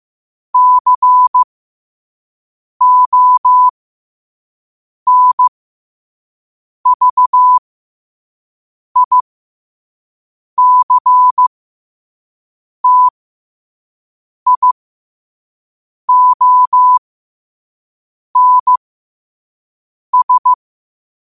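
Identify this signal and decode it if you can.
Morse code, heard as a keyed tone, about 15 wpm, spaced as for 5 wpm, 1000 Hz. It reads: CONVICTIONS
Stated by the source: Morse code sent at 5 wpm.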